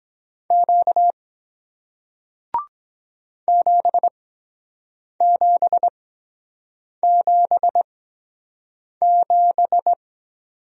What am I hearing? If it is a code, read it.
Morse code, heard as a keyed tone, about 23 wpm, spaced as for 7 wpm, 700 Hz. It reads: Q 7777